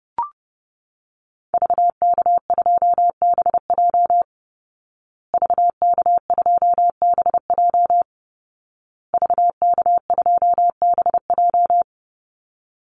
Morse code, heard as a keyed tone, about 30 wpm, 700 Hz. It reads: VK2BJ VK2BJ VK2BJ
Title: Call Signs Repeated 2x - 30wpm